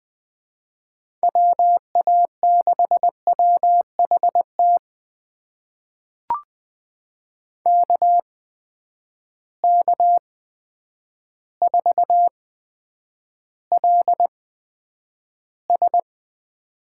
Morse code, heard as a keyed tone, about 20 wpm, spaced as for 9 wpm, 700 Hz. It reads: WA6WHT KK4LS
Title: USA Call Signs - 8x Character Spacing - 20wpm